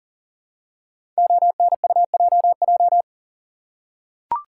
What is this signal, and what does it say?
Morse code, heard as a keyed tone, about 40 wpm, 700 Hz. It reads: ONUJJ
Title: Code Groups - 5 Letters - 40wpm